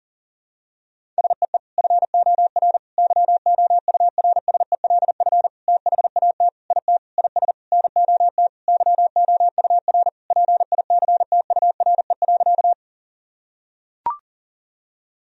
Morse code, heard as a keyed tone, about 40 wpm, 700 Hz. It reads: SEE FOR YOURSELF THAT IT IS NOT YOUR PICTURE.
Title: Sentences from Top 1000 Words - Review Part 1 - 40wpm